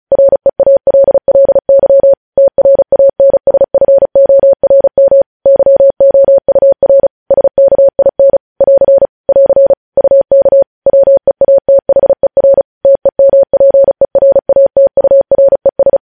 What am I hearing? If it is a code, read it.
Morse code, heard as a keyed tone, about 36 wpm, 550 Hz. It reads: REALLY TRANSFORM YOUR SKIN <AR> <AR> UK WEATHER TEMPERATURES
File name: podcast / News headlines-1x-35x35